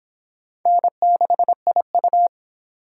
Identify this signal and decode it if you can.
Morse code, heard as a keyed tone, about 28 wpm, 700 Hz. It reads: N6IU